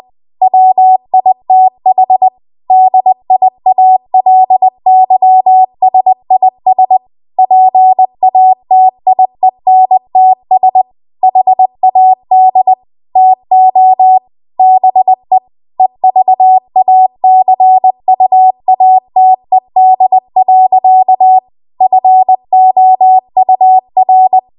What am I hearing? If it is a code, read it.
Morse code, heard as a keyed tone, about 20 wpm, 750 Hz. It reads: WITH DIALYSIS PATIENTS HAD TO BE EVACUATED. FOUR